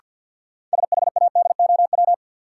Morse code, heard as a keyed tone, about 50 wpm, 700 Hz. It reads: SHADOW
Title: Sets of 5 Words Encore - New Word Drill - 50wpm